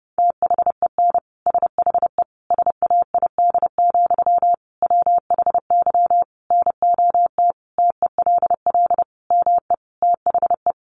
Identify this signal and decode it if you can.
Morse code, heard as a keyed tone, about 30 wpm, 700 Hz. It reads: THEN SHE SAID, WHY NOT TELL ME THE
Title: Sentences from Top 800 Words 30wpm